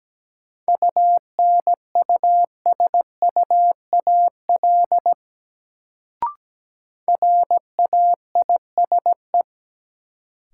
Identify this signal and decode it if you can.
Morse code, heard as a keyed tone, about 17 wpm, 700 Hz. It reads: UNUSUAL RAISE
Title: Sentences from Top 1400 Words - New Word Drill - 17wpm